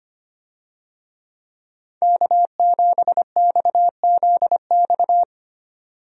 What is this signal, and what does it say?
Morse code, heard as a keyed tone, about 25 wpm, 700 Hz. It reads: K7XZX
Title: USA Call Signs - 5 Characters - 25wpm